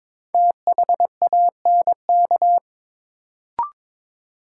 Morse code, heard as a keyed tone, about 21 wpm, 700 Hz. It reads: THANK
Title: Sets of 2 Words - New Word Drill - 22wpm